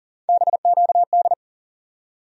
Morse code, heard as a keyed tone, about 40 wpm, 700 Hz. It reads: BQD
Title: USA Call Sign Postfixes - 3 Letters 40wpm